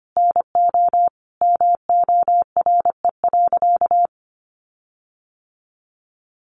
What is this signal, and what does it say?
Morse code, heard as a keyed tone, about 25 wpm, 700 Hz. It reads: NO MORE.
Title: Sentences from Top 1400 Words 25wpm